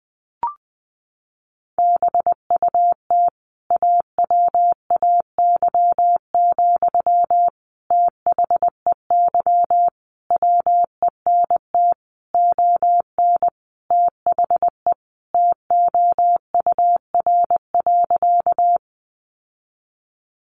Morse code, heard as a keyed tone, about 19 wpm, 700 Hz. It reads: BUT AWAY, THEY WENT ON THE TOUR.
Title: Sentences from Top 1400 Words 20wpm